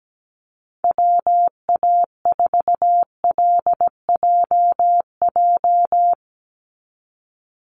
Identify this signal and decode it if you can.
Morse code, heard as a keyed tone, about 17 wpm, 700 Hz. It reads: WA4LJJ
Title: Call Signs - 17wpm